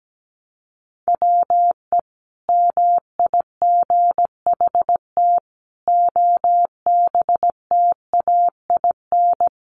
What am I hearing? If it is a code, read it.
Morse code, heard as a keyed tone, about 17 wpm, 700 Hz. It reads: WE MIGHT OBTAIN
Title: Sentences from Top 1300 Words 17wpm